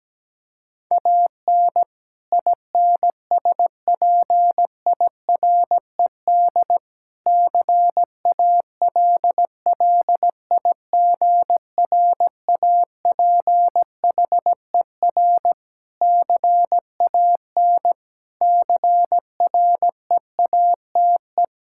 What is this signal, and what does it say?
Morse code, heard as a keyed tone, about 17 wpm, 700 Hz. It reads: AN INSPIRED CALLIGRAPHER CAN CREATE
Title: Pangram Challenge v2 - 17wpm